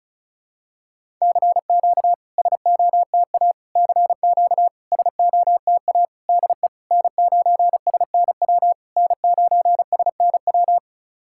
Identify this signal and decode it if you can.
Morse code, heard as a keyed tone, about 36 wpm, 700 Hz. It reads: CQ SOTA CQ SOTA DE N9SNW N9SNW
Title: SOTA QSO Practice 35wpm